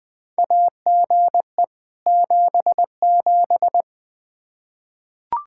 Morse code, heard as a keyed tone, about 20 wpm, 700 Hz. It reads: AGE 77
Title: Practice QSO - Exchange 3 - More Info - 20wpm